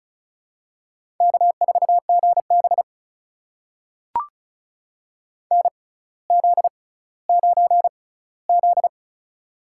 Morse code, heard as a keyed tone, about 36 wpm, spaced as for 18 wpm, 700 Hz. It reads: K4GB NZ9Z
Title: USA Call Signs - 4 Characters - 6x Character Spacing - 35wpm